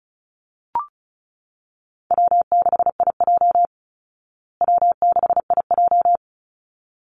Morse code, heard as a keyed tone, about 34 wpm, 700 Hz. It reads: W6IJ W6IJ
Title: Easy Call Signs Repeated 2x - 35wpm